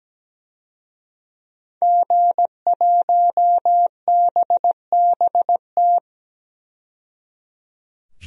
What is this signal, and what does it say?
Morse code, heard as a keyed tone, about 17 wpm, 700 Hz. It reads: G1BBT
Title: DXCC-100 Call Signs - 5 Characters - 17wpm